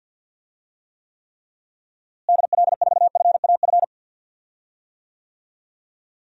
Morse code, heard as a keyed tone, about 49 wpm, 700 Hz. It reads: DL4FAF
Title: DXCC-100 Call Signs - 6 Characters - Repeated 3x - 50wpm